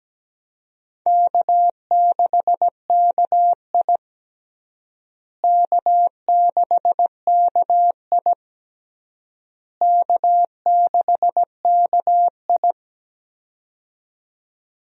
Morse code, heard as a keyed tone, about 17 wpm, 700 Hz. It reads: K6KI K6KI K6KI
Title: USA Call Signs - 4 Characters - Repeated 3x - 17wpm